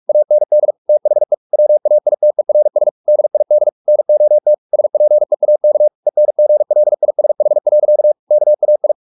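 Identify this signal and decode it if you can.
Morse code, heard as a keyed tone, about 44 wpm, 600 Hz. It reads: AND THE WAITERS DID NOT SPEAK ENGLISH. KAI
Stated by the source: Morse practice at 45 wpm, the novel Never by Ken Follett in English, from CW med Gnister.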